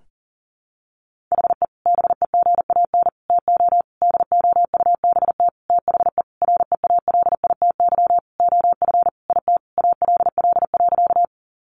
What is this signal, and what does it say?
Morse code, heard as a keyed tone, about 40 wpm, 700 Hz. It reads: HE BEGAN TO DOUBT THE REALITY OF IT ALL.
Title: Sentences from Top 1500 Words 40wpm